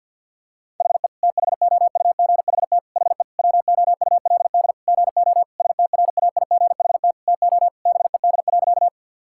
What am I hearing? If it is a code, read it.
Morse code, heard as a keyed tone, about 45 wpm, 700 Hz. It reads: HE THOUGHT HE WOULD GO STRAIGHT TO BED.
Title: Sentences from Top 1000 Words - Review Part 6 - 50wpm